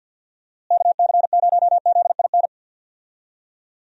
Morse code, heard as a keyed tone, about 47 wpm, 700 Hz. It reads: KC0ZIN
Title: USA Call Signs - 6 Characters - Repeated 3x - 50wpm